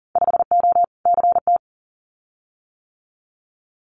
Morse code, heard as a keyed tone, about 38 wpm, 700 Hz. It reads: 5O CT
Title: Winter Field Day Exchanges Encore 40wpm